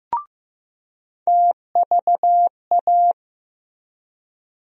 Morse code, heard as a keyed tone, about 15 wpm, 700 Hz. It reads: TVA
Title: USA Call Sign Postfixes - 3 Letters 15wpm